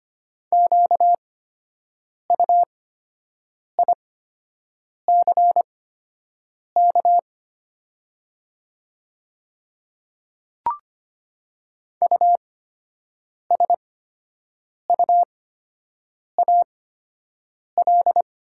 Morse code, heard as a keyed tone, about 25 wpm, spaced as for 7 wpm, 700 Hz. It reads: QUICK USUAL
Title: 8x Character Spacing - Top 500 Words - 25wpm